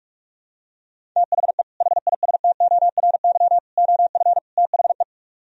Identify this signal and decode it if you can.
Morse code, heard as a keyed tone, about 48 wpm, 700 Hz. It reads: THE HISTORY OF THE